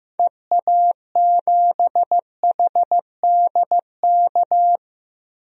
Morse code, heard as a keyed tone, about 15 wpm, 700 Hz. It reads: EA7HDK
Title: DXCC-100 Call Signs - 6 Characters - Repeated 3x - 15wpm